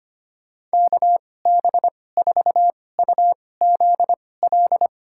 Morse code, heard as a keyed tone, about 25 wpm, spaced as for 18 wpm, 690 Hz. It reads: KB4UZL
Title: USA Call Signs - 6 Characters - 2x Character Spacing - 25wpm